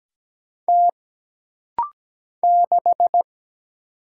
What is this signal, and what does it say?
Morse code, heard as a keyed tone, about 17 wpm, 700 Hz. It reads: T 6
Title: Single Letter-Number - Rapid-Fire 17wpm